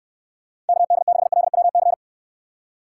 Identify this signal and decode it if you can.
Morse code, heard as a keyed tone, about 64 wpm, 700 Hz. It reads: KD6LWC